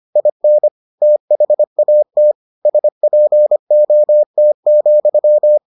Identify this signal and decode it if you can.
Morse code, heard as a keyed tone, about 25 wpm, 600 Hz. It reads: IN THAT SPOT,